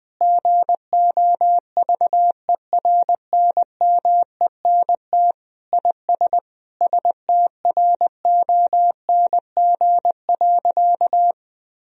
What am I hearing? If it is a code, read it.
Morse code, heard as a keyed tone, about 20 wpm, 700 Hz. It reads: GOVERNMENT IS STRONG.
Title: Sentences from Top 1000 Words - Review Part 4 - 20wpm